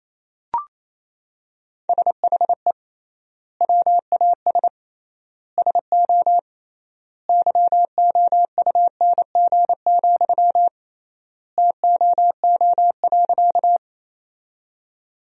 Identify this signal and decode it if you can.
Morse code, heard as a keyed tone, about 28 wpm, 700 Hz. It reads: SHE WAS SO YOUNG, TOO.
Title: 3x Word-Spacing - Sentences from Top 400 Words - 28wpm